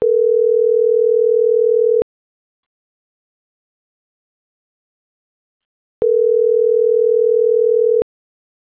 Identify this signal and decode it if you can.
Ringback tone (North America), 440 + 480 Hz.